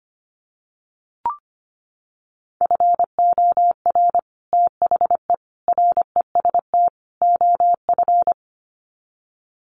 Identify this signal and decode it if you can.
Morse code, heard as a keyed tone, about 25 wpm, 700 Hz. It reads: FOR THE REST OF
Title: Sets of 4 Words 25wpm